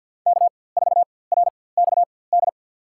Morse code, heard as a keyed tone, about 48 wpm, spaced as for 23 wpm, 700 Hz. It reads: K4RXD
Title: USA Call Signs - 5 Characters - 4x Character Spacing - 50wpm